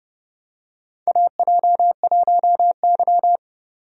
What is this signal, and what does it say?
Morse code, heard as a keyed tone, about 29 wpm, 700 Hz. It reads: AJ1Y
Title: USA Call Signs - 4 Characters - 30wpm